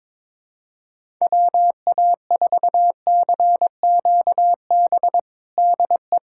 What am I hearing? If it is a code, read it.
Morse code, heard as a keyed tone, about 22 wpm, 700 Hz. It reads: WA4CQB DE
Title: USA Callsign DE Callsign - 6 Characters - 22wpm